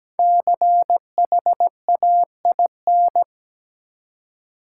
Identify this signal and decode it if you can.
Morse code, heard as a keyed tone, about 17 wpm, 700 Hz. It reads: CHAIN
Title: Five Letter Words - 17wpm